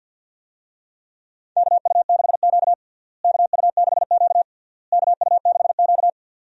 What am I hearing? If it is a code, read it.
Morse code, heard as a keyed tone, about 47 wpm, 700 Hz. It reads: KU6Q KU6Q KU6Q
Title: USA Call Signs - 4 Characters - Repeated 3x - 50wpm